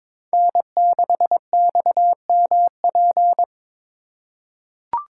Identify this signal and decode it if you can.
Morse code, heard as a keyed tone, about 21 wpm, 700 Hz. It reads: N6XMP
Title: USA Call Signs - 5 Characters - 3x Character Spacing - 22wpm